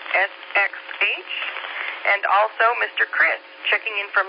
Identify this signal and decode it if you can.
Emergency Alert System (SAME) tones, heard as harsh buzzing bursts, with speech. It is heard from 0.4 to 3.4 s.
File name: qsorec SimplexLogic